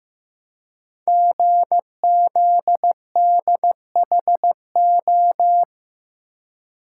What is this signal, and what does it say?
Morse code, heard as a keyed tone, about 15 wpm, 700 Hz. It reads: GZDHO
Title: Code Groups - 5 Letters - 15wpm